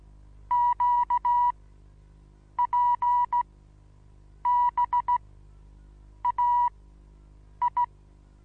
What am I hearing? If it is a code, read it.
Morse code, heard as a keyed tone, about 16 wpm, spaced as for 7 wpm, 970 Hz. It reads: QPBAI
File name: Onlinelehrgang / morsen / DJ4UF Morsen25